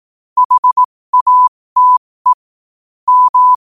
Morse code, heard as a keyed tone, about 18 wpm, spaced as for 15 wpm, 1000 Hz. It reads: HATE M
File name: qotdfiles / 15wpm